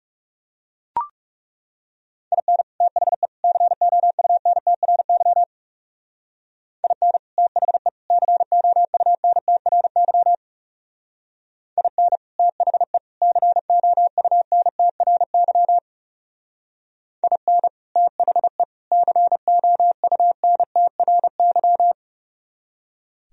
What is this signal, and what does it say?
Morse code, heard as a keyed tone, about 37 wpm, 700 Hz. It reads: IN THE COUNTRY IN THE COUNTRY IN THE COUNTRY IN THE COUNTRY
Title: Speed-Racing - Sets of 3 Words 30wpm